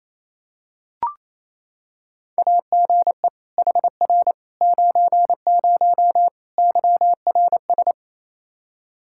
Morse code, heard as a keyed tone, about 28 wpm, 700 Hz. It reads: AGE HR 90 YRS
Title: Practice QSO - Exchange 3 - More Info - 28wpm